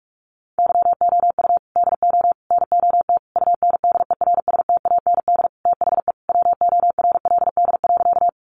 Morse code, heard as a keyed tone, about 44 wpm, 700 Hz. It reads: YOU DO NOT UNDERSTAND THE WORLD.
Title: Sentences from Top 600 Words 45wpm